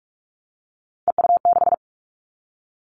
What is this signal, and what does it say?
Morse code, heard as a keyed tone, about 46 wpm, 700 Hz. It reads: EU6
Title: Callsign Prefixes for Top 50 DXCC - 45wpm